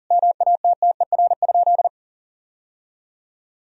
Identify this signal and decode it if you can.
Morse code, heard as a keyed tone, about 37 wpm, 700 Hz. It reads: MATTER?
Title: Sentences from Top 1000 Words - Review Part 1 - 40wpm